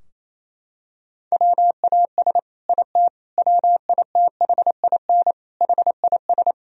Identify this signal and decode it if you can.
Morse code, heard as a keyed tone, about 28 wpm, 700 Hz. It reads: WAS IT WITHIN HIS
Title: Sentences from Top 1000 Words - Review Part 4 - 28wpm